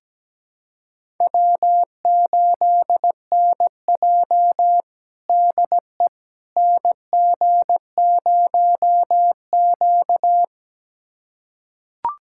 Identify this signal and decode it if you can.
Morse code, heard as a keyed tone, about 17 wpm, 700 Hz. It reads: W8NJ DE NG0Q
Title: USA Callsign DE Callsign - 4 Characters - 17wpm